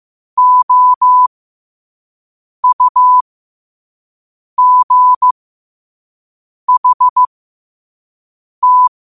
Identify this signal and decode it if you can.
Morse code, heard as a keyed tone, about 15 wpm, spaced as for 5 wpm, 1000 Hz. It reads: OUGHT
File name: qotdfiles / 5wpm